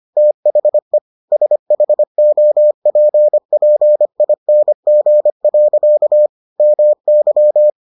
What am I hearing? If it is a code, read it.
Morse code, heard as a keyed tone, about 25 wpm, 600 Hz. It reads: THE SHOPPING. MY